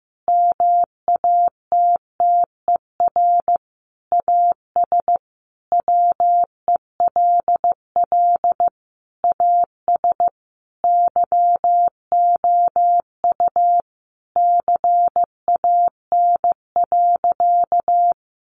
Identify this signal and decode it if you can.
Morse code, heard as a keyed tone, about 15 wpm, 700 Hz. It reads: MATTER AS WELL AS YOU CAN.
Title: Sentences from Top 1400 Words 15wpm